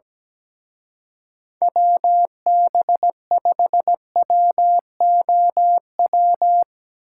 Morse code, heard as a keyed tone, about 18 wpm, 700 Hz. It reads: WB5WOW
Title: Speed Racing - USA 2x3 Call Signs 17wpm